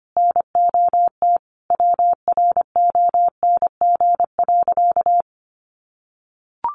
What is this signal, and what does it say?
Morse code, heard as a keyed tone, about 25 wpm, 700 Hz. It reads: NOT WRONG.
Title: Sentences from Top 900 Words 25wpm